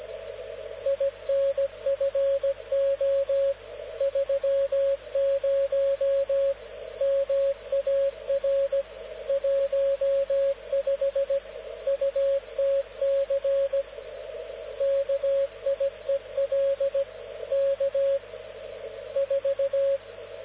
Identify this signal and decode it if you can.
Morse code, heard as a keyed tone, about 17 wpm, 550 Hz, with radio noise. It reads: INFO 30 MAR 15 UTC KIEL K 4
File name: luxorion / Radio / dk0wcy-spaceweather-10144khz